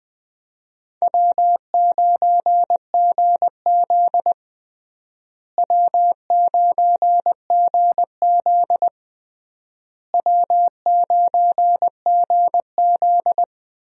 Morse code, heard as a keyed tone, about 20 wpm, 700 Hz. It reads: W9GZ W9GZ W9GZ
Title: USA Call Signs - 4 Characters - Repeated 3x - 20wpm